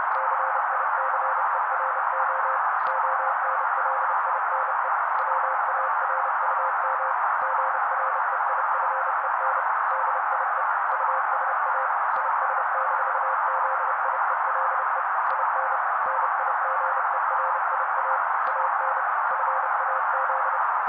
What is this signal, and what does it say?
Morse code, heard as a keyed tone, about 30 wpm, 530 Hz. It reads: CICERO ONLINE WARUM GREIFEN DIE USA IS-ZIELE IN NIGERIA AN FAZ
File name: daily / 29wpm noise